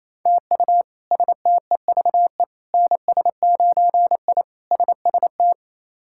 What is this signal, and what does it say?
Morse code, heard as a keyed tone, about 28 wpm, 700 Hz. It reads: TU STEVE NS9I SST
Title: K1USN SST QSO Practice - 28wpm